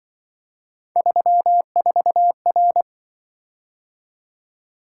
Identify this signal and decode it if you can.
Morse code, heard as a keyed tone, about 24 wpm, 700 Hz. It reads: 34R
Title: Speed-Racing - Three Letter-Number Combinations - 20wpm